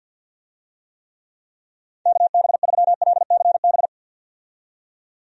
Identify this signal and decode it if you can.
Morse code, heard as a keyed tone, about 50 wpm, 700 Hz. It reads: KB3LCB
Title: USA Call Signs - 6 Characters - 50wpm